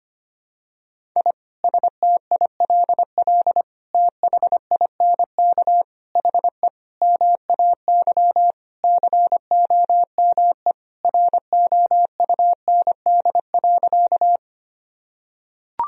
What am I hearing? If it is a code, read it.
Morse code, heard as a keyed tone, about 25 wpm, 700 Hz. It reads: I STILL THINK HE MAY COME ROUND.
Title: Sentences from Top 1000 Words - Review Part 1 - 25wpm